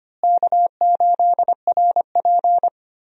Morse code, heard as a keyed tone, about 25 wpm, 700 Hz. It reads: K8RP
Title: USA Call Signs - 4 Characters - Repeated 3x - 25wpm